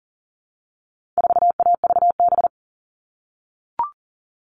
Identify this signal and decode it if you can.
Morse code, heard as a keyed tone, about 40 wpm, 700 Hz. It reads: 4AVB
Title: All Cut Numbers - 4-Digit 40wpm